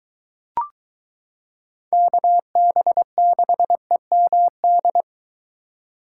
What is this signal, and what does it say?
Morse code, heard as a keyed tone, about 22 wpm, 700 Hz. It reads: KB6EMD